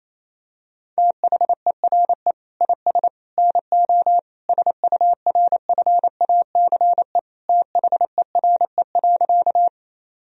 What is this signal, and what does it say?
Morse code, heard as a keyed tone, about 28 wpm, 700 Hz. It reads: THERE IS NO SURFACE THERE.
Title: Sentences from Top 1000 Words - Review Part 3 - 28wpm